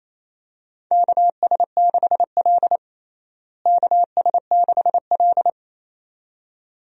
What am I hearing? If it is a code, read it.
Morse code, heard as a keyed tone, about 28 wpm, 700 Hz. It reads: KS6L KS6L
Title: USA Call Signs - 4 Characters - Repeated 2x - 28wpm